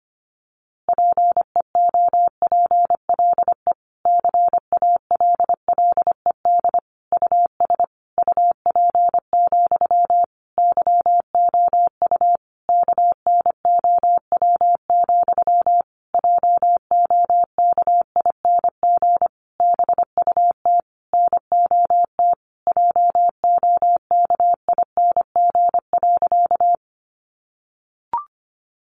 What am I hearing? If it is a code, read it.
Morse code, heard as a keyed tone, about 25 wpm, 700 Hz. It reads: PEOPLE CALLED US UP, YOU KNOW, JOKING BUT NOT JOKING.
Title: A Sound Of Thunder 25wpm